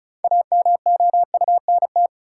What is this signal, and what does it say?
Morse code, heard as a keyed tone, about 34 wpm, 700 Hz. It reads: AMOUNT
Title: Sets of 3 Words - New Word Drill - 35wpm